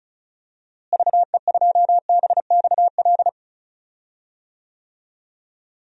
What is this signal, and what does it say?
Morse code, heard as a keyed tone, about 36 wpm, 700 Hz. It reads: VE2BXL
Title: DXCC-100 Call Signs - 6 Characters - Repeated 2x - 35wpm